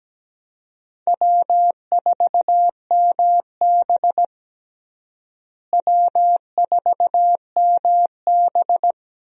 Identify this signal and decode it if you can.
Morse code, heard as a keyed tone, about 17 wpm, 700 Hz. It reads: W4MB W4MB
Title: USA Call Signs - 4 Characters - Repeated 2x - 17wpm